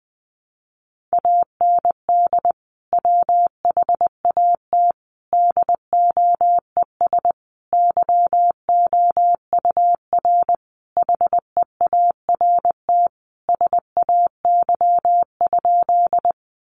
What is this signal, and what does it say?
Morse code, heard as a keyed tone, about 20 wpm, 700 Hz. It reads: AND WHAT DOES YOUR HEART SAY?
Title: Sentences from Top 1200 Words 20wpm